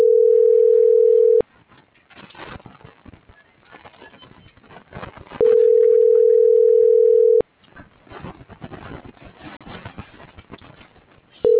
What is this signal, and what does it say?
Ringback tone (North America), 440 + 480 Hz.